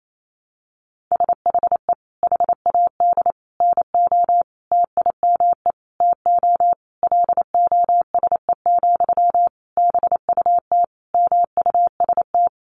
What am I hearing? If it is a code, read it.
Morse code, heard as a keyed tone, about 28 wpm, 700 Hz. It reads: SHE HAD NO TIME TO LOSE, BUT MUST
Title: Sentences from Top 1100 Words 28wpm